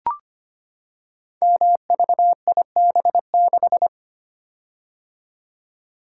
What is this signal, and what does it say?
Morse code, heard as a keyed tone, about 25 wpm, 700 Hz. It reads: MVIB6
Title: Code Groups - 5 Letter:Numbers - 25wpm